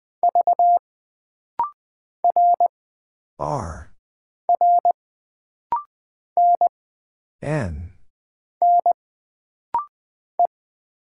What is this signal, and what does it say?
Morse code, heard as a keyed tone, about 20 wpm, 700 Hz. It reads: V R R N N E
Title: Single Character (Letter:Number:Punctuation:Prosign) - Rapid-Fire - 20wpm